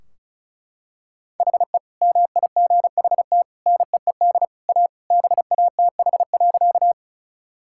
Morse code, heard as a keyed tone, about 34 wpm, 700 Hz. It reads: HE MIGHT NEED A BATH.
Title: Sentences from Top 1600 Words 35wpm